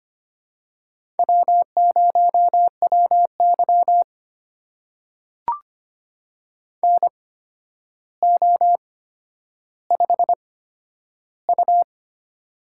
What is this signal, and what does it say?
Morse code, heard as a keyed tone, about 25 wpm, spaced as for 11 wpm, 700 Hz. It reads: W0WY NO5U
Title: USA Call Signs - 4 Characters - 8x Character Spacing - 25wpm